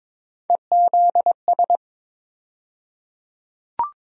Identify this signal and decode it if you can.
Morse code, heard as a keyed tone, about 22 wpm, 700 Hz. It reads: EZS
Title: USA Call Sign Postfixes - 3 Letters 22wpm